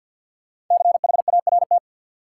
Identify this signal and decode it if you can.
Morse code, heard as a keyed tone, about 52 wpm, 700 Hz. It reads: CHART
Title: Sentences from Top 1300 Words - New Word Drill - 50wpm